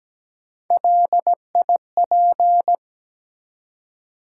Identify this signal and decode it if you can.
Morse code, heard as a keyed tone, about 17 wpm, 700 Hz. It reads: LIP
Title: Three Letter Words - 17wpm